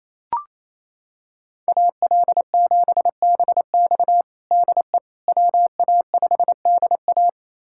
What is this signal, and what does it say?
Morse code, heard as a keyed tone, about 28 wpm, 700 Hz. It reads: AL7BX DE WA5DA
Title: USA Callsign DE Callsign - 5 Characters - 28wpm